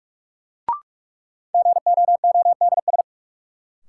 Morse code, heard as a keyed tone, about 45 wpm, 700 Hz. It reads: GOODS
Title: Sentences from Top 1000 Words - Review Part 6 - New Word Drill - 45wpm